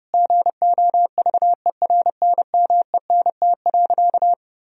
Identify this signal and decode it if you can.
Morse code, heard as a keyed tone, about 32 wpm, 700 Hz. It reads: GOVERNMENT.
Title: Sentences from Top 1000 Words - Review Part 3 - 30wpm